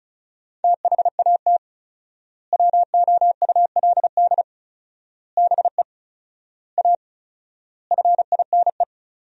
Morse code, heard as a keyed tone, about 34 wpm, 700 Hz. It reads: THAT WOULD BE A FINE
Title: 4x Word-Spacing - Sentences from Top 500 Words - 35wpm